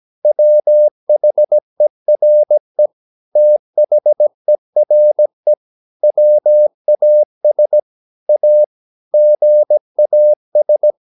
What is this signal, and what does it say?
Morse code, heard as a keyed tone, about 17 wpm, 600 Hz. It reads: WHERE THERE WAS A GAS
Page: Romanen Never af Ken Follet på engelsk.